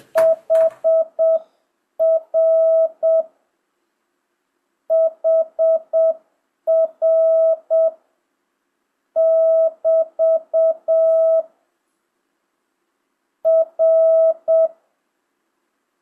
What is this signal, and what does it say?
Morse code, heard as a keyed tone, about 7 wpm, 650 Hz. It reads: HR HR <BT> R
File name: jh8cbh / 2025cw